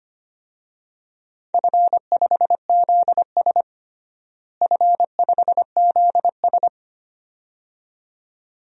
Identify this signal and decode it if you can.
Morse code, heard as a keyed tone, about 25 wpm, 700 Hz. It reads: F5ZS F5ZS
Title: DXCC-100 Call Signs - 4 Characters - Repeated 2x - 25wpm